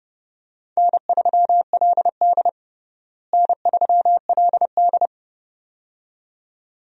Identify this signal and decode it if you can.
Morse code, heard as a keyed tone, about 30 wpm, 700 Hz. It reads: N3LD N3LD
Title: USA Call Signs - 4 Characters - Repeated 2x - 30wpm